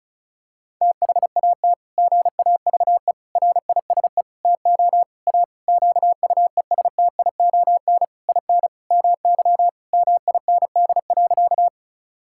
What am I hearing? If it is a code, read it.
Morse code, heard as a keyed tone, about 36 wpm, 700 Hz. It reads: THAT GAVE RISE TO A QUESTION IN MY MIND.
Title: Sentences from Top 1000 Words - Review Part 6 - 35wpm